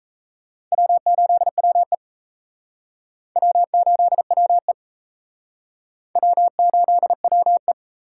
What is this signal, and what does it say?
Morse code, heard as a keyed tone, about 33 wpm, 700 Hz. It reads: W8WE W8WE W8WE
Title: Speed Racing - USA 1x2 Call Signs 28wpm